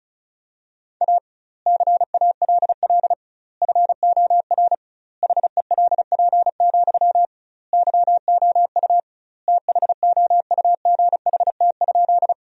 Morse code, heard as a keyed tone, about 34 wpm, 700 Hz. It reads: A CALL FOR HELP, YOU THOUGHT?
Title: 2x Word-Spacing - Sentences from Top 300 Words - 35wpm